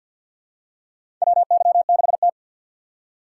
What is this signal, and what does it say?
Morse code, heard as a keyed tone, about 46 wpm, 700 Hz. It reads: WY6T
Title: USA Call Signs - 4 Characters - 2x Character Spacing - 50wpm